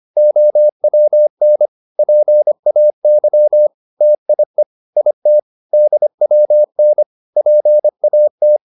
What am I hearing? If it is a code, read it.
Morse code, heard as a keyed tone, about 25 wpm, 600 Hz. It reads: OWN PAY TIE IT DWN PAT